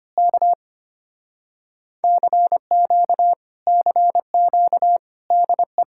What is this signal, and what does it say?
Morse code, heard as a keyed tone, about 24 wpm, 700 Hz. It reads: K CQ CQ DE
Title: Speed-Racing Common QSO Phrases - 25wpm